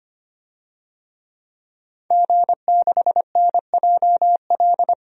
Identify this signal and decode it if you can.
Morse code, heard as a keyed tone, about 25 wpm, 700 Hz. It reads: G6NJL
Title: DXCC-100 Call Signs - 5 Characters - 25wpm